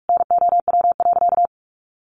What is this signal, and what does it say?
Morse code, heard as a keyed tone, about 44 wpm, 700 Hz. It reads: NOW.
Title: Sentences from Top 1500 Words 45wpm